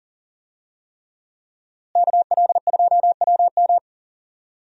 Morse code, heard as a keyed tone, about 40 wpm, 700 Hz. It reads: KL2WM
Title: USA Call Signs - 5 Characters - Repeated 3x - 40wpm